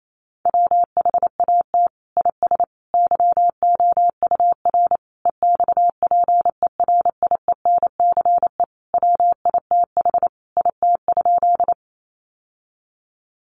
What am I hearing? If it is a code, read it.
Morse code, heard as a keyed tone, about 28 wpm, 700 Hz. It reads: WHAT IS YOUR EXPERIENCE WITH IT?
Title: Sentences from Top 600 Words 28wpm